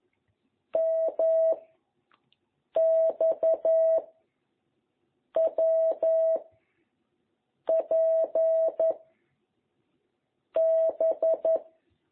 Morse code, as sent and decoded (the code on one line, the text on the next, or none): -- / -..- / .-- / .--. / -...
M X W P B